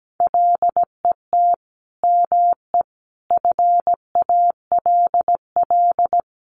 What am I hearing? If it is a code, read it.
Morse code, heard as a keyed tone, about 17 wpm, 700 Hz. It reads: LET ME FALL
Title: Sentences from Top 1100 Words 17wpm